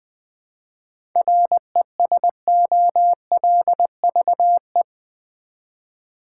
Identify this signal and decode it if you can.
Morse code, heard as a keyed tone, about 20 wpm, 700 Hz. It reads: RESOLVE
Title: 7 Letter Words - First 3 Letters Given - 20wpm